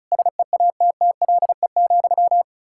Morse code, heard as a keyed tone, about 45 wpm, 700 Hz. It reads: SEATTLE,